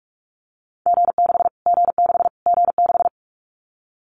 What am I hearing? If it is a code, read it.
Morse code, heard as a keyed tone, about 44 wpm, 700 Hz. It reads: G6 G6 G6
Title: Callsign Prefixes for Top 25 DXCC - Repeated 3x - 45wpm